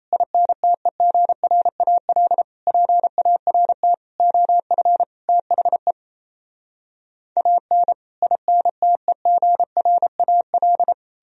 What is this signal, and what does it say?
Morse code, heard as a keyed tone, about 38 wpm, 700 Hz. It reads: INTEGRAL PART OF THE AN INTEGRAL